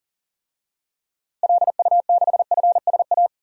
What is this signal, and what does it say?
Morse code, heard as a keyed tone, about 41 wpm, 700 Hz. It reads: LU6FSA
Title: DXCC-100 Call Signs - 6 Characters - Repeated 3x - 40wpm